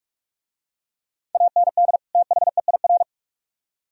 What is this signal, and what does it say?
Morse code, heard as a keyed tone, about 43 wpm, 700 Hz. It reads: AND THEIR